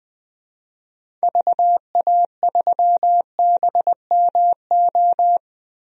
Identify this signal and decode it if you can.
Morse code, heard as a keyed tone, about 20 wpm, 700 Hz. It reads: VA3BMO
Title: DXCC-100 Call Signs - 6 Characters - 20wpm